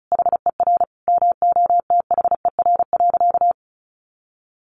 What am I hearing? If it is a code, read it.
Morse code, heard as a keyed tone, about 34 wpm, 700 Hz. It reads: HER MOTHER.
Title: Sentences from Top 1100 Words 35wpm